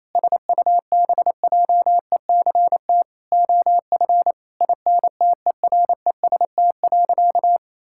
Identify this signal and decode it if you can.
Morse code, heard as a keyed tone, about 28 wpm, 700 Hz. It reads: SUBJECT OF INTEREST.
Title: Sentences from Top 1000 Words - Review Part 5 - 28wpm